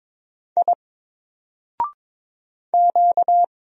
Morse code, heard as a keyed tone, about 22 wpm, 700 Hz. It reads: I Q
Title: Single Character (Letter:Number:Punctuation:Prosign) Alt 22wpm